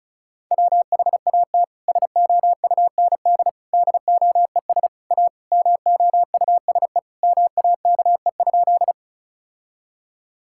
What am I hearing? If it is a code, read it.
Morse code, heard as a keyed tone, about 36 wpm, 700 Hz. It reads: WHAT SOUND DOES A MOUSE MAKE?
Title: Sentences from Top 1000 Words - Review Part 5 - 35wpm